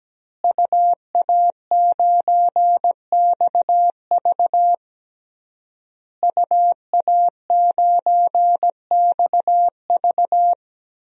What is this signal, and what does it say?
Morse code, heard as a keyed tone, about 17 wpm, 700 Hz. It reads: UA9XV UA9XV
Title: DXCC-100 Call Signs - 5 Characters - Repeated 3x - 17wpm